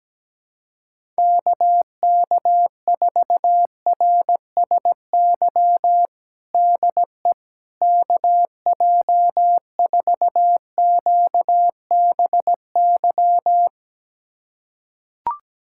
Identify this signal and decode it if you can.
Morse code, heard as a keyed tone, about 17 wpm, 700 Hz. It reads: KK4RSY DE KJ4QBY
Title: USA Callsign DE Callsign - 6 Characters - 17wpm